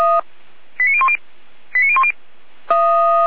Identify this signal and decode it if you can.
Five-tone selective call (ZVEI), heard as quick runs of five tones, tones of about 70 ms. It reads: 89010 89010
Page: SDR/ZVEI